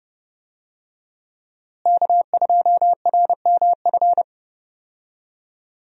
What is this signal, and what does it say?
Morse code, heard as a keyed tone, about 32 wpm, 700 Hz. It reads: K2RMF
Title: USA Call Signs - 5 Characters - Repeated 2x - 30wpm